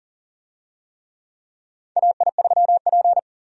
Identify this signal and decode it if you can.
Morse code, heard as a keyed tone, about 37 wpm, 700 Hz. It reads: AI3P
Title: USA Call Signs - 4 Characters - Repeated 3x - 40wpm